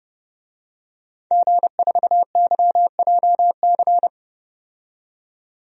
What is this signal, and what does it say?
Morse code, heard as a keyed tone, about 32 wpm, 700 Hz. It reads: G4YJC
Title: DXCC-100 Call Signs - 5 Characters - 30wpm